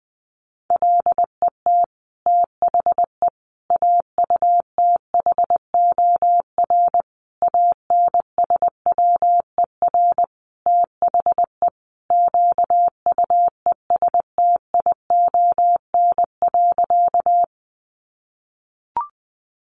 Morse code, heard as a keyed tone, about 19 wpm, 700 Hz. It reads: LET THE AUTHOR ANSWER THE QUESTION.
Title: Sentences from Top 1200 Words 20wpm